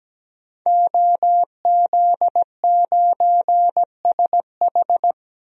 Morse code, heard as a keyed tone, about 17 wpm, 700 Hz. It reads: OZ9SH
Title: DXCC-100 Call Signs - 5 Characters - 17wpm